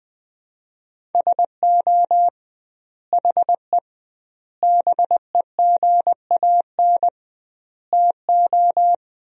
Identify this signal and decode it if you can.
Morse code, heard as a keyed tone, about 20 wpm, 700 Hz. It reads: SO HE BEGAN TO
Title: 2x Word-Spacing - Sentences from Top 300 Words - 20wpm